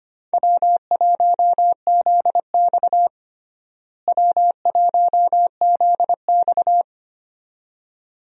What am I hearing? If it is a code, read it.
Morse code, heard as a keyed tone, about 25 wpm, 700 Hz. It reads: W1ZX W1ZX
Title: USA Call Signs - 4 Characters - Repeated 2x - 25wpm